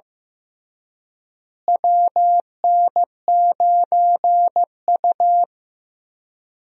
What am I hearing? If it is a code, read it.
Morse code, heard as a keyed tone, about 16 wpm, 700 Hz. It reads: WN9U
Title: USA Call Signs - 4 Characters - Repeated 2x - 15wpm